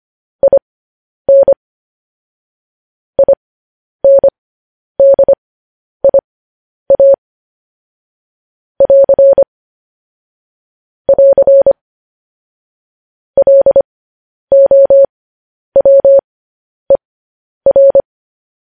Morse code, heard as a keyed tone, about 25 wpm, spaced as for 10 wpm, 550 Hz. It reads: IN INDIA <AR> <AR> LOWER
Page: news-headlines-in-morse-code-at-10-wpm